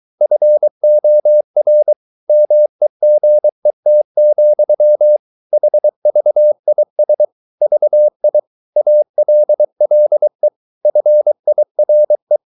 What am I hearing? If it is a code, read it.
Morse code, heard as a keyed tone, about 23 wpm, 600 Hz. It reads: FOR MEGET, HVIS VI ALLE FIRE
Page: Ferie 23 wpm